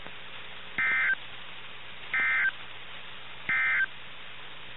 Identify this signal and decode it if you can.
Emergency Alert System (SAME) tones, heard as harsh buzzing bursts. It is heard from 0.7 to 3.9 s.
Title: NOAA Weather Radio - KIH61